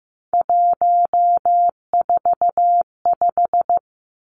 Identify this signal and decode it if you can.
Morse code, heard as a keyed tone, about 15 wpm, 700 Hz. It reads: 145
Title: Numbers - 3-Digit 15wpm